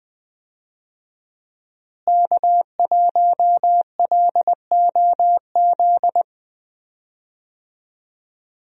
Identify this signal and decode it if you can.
Morse code, heard as a keyed tone, about 20 wpm, 700 Hz. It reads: K1LOZ
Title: USA Call Signs - 5 Characters - Repeated 3x - 20wpm